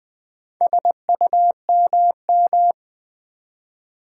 Morse code, heard as a keyed tone, about 20 wpm, 700 Hz. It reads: SUMM
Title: Build A Word - 2 Letters At A Time - 6-Letter Words - 20wpm